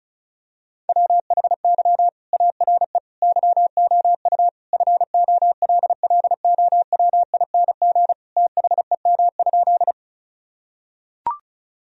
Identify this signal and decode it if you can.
Morse code, heard as a keyed tone, about 36 wpm, 700 Hz. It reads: WHY ARE YOU FOLLOWING THEM?
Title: Sentences from Top 1000 Words - Review Part 4 - 35wpm